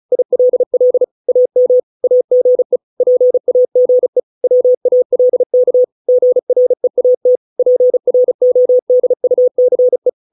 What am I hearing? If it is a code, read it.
Morse code, heard as a keyed tone, about 34 wpm, 500 Hz. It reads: ILL AM AGE PAGE WALK GREAT PRODUCE